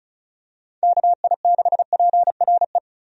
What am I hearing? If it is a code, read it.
Morse code, heard as a keyed tone, about 34 wpm, 700 Hz. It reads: KI6PRE
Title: USA Call Signs - 6 Characters - 35wpm